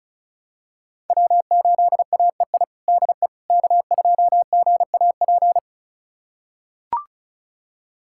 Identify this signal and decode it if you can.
Morse code, heard as a keyed tone, about 36 wpm, 700 Hz. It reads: W8AEI DE K2GAP
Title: USA Callsign DE Callsign - 5 Characters - 35wpm